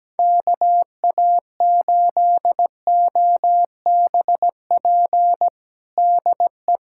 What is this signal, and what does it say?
Morse code, heard as a keyed tone, about 17 wpm, 700 Hz. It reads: KA8OBP DE
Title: USA Callsign DE Callsign - 6 Characters - 17wpm